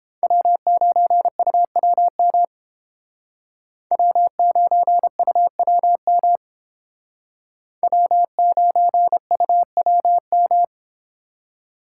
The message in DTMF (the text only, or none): none